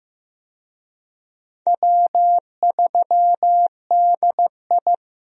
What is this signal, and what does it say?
Morse code, heard as a keyed tone, about 15 wpm, 700 Hz. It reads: W3DI
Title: USA Call Signs - 4 Characters - 15wpm